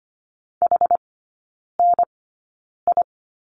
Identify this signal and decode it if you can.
Morse code, heard as a keyed tone, about 25 wpm, spaced as for 9 wpm, 700 Hz. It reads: HNI
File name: Lesson 14e - Number 2 - Three Characters 25wpm